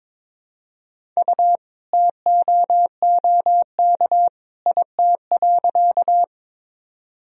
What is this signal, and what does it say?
Morse code, heard as a keyed tone, about 22 wpm, 700 Hz. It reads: U TOOK IT.
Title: Sentences from Top 300 Words and Abbreviations 22wpm